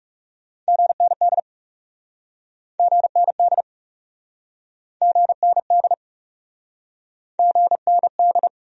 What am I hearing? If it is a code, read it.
Morse code, heard as a keyed tone, about 42 wpm, 700 Hz. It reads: GND GND GND GND